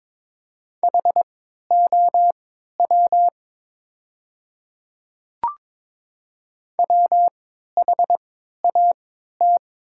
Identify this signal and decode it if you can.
Morse code, heard as a keyed tone, about 22 wpm, spaced as for 12 wpm, 700 Hz. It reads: HOW WHAT
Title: 3x Character Spacing - Top 300 Words - 22wpm